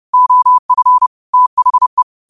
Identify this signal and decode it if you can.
Morse code, heard as a keyed tone, about 30 wpm, 1000 Hz. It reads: OF THE
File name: qotdfiles / 30wpm